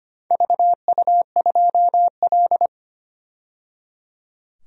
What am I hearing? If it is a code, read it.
Morse code, heard as a keyed tone, about 25 wpm, 700 Hz. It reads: VU2L
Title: DXCC-100 Call Signs - 4 Characters - 25wpm